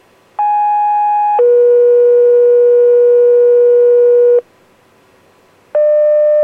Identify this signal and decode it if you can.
Two-tone page, heard as a one-second tone, then a three-second tone, about 830 then 480 Hz. It is heard from 0.4 to 4.4 s.